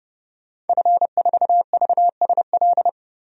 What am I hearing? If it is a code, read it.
Morse code, heard as a keyed tone, about 30 wpm, 700 Hz. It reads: F4VSL
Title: DXCC-100 Call Signs - 5 Characters - Repeated 2x - 30wpm